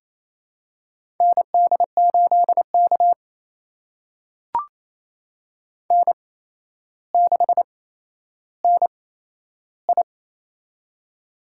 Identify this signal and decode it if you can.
Morse code, heard as a keyed tone, about 28 wpm, spaced as for 12 wpm, 700 Hz. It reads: ND8K N6NI
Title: USA Call Signs - 4 Characters - 8x Character Spacing - 28wpm